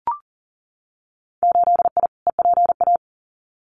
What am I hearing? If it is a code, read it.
Morse code, heard as a keyed tone, about 40 wpm, 700 Hz. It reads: 8I EPA
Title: Winter Field Day Exchanges Encore 40wpm